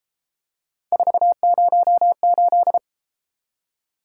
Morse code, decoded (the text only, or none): 408